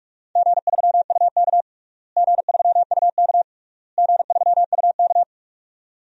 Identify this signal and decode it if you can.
Morse code, heard as a keyed tone, about 44 wpm, 700 Hz. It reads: G3UK G3UK G3UK
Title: DXCC-100 Call Signs - 4 Characters - Repeated 3x - 45wpm